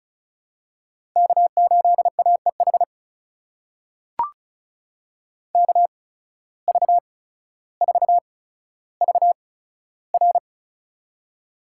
Morse code, heard as a keyed tone, about 36 wpm, spaced as for 15 wpm, 700 Hz. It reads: K8AEH KV4VR
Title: USA Call Signs - 8x Character Spacing - 35wpm